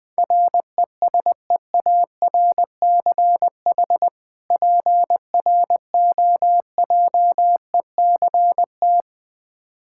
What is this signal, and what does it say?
Morse code, heard as a keyed tone, about 20 wpm, 700 Hz. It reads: RESEARCH PROJECT
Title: Difficult Words - Speed-Racing with Hints - 20wpm